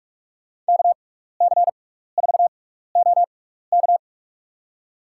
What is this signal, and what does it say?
Morse code, heard as a keyed tone, about 46 wpm, spaced as for 16 wpm, 700 Hz. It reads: KC4OK
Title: USA Call Signs - 6x Character Spacing - 45wpm